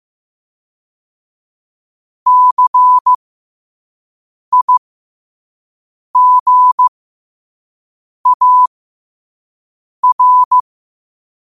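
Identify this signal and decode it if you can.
Morse code, heard as a keyed tone, about 15 wpm, spaced as for 5 wpm, 1000 Hz. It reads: CIGAR
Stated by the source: Morse quote of the day at 5 wpm.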